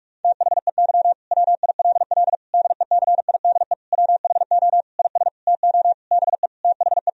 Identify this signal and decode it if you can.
Morse code, heard as a keyed tone, about 46 wpm, 700 Hz. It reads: THEY WILL DECIDE WHO IS TO BE THE
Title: Sentences from Top 1000 Words - Review Part 3 - 45wpm